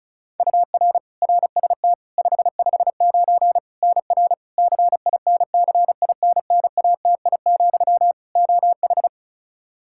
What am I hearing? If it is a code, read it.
Morse code, heard as a keyed tone, about 36 wpm, 700 Hz. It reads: UR RST 559 NR CINCINNATI, OH
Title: K1USN SST QSO Practice 35wpm